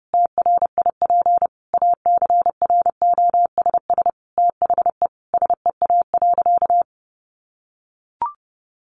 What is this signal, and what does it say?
Morse code, heard as a keyed tone, about 30 wpm, 700 Hz. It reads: TRIP ACROSS THE SEA.
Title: Sentences from Top 1300 Words 30wpm